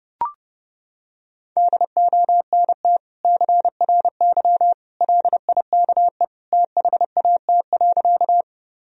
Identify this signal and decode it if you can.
Morse code, heard as a keyed tone, about 30 wpm, 700 Hz. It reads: DONT CRY LIKE THAT.
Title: Sentences from Top 1000 Words - Review Part 2 - 30wpm